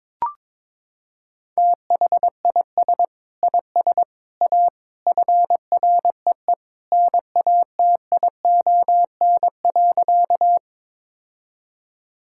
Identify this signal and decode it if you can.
Morse code, heard as a keyed tone, about 22 wpm, 700 Hz. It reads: THIS IS A FREE NATION.
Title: Sentences from Top 1000 Words - Review Part 6 - 22wpm